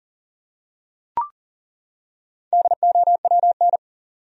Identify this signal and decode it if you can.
Morse code, heard as a keyed tone, about 40 wpm, 700 Hz. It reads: DOWN
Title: Top 100 Words 40wpm